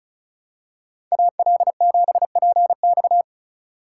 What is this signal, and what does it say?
Morse code, heard as a keyed tone, about 36 wpm, 700 Hz. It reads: AL7PX
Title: USA Call Signs - 2x Character Spacing - 35wpm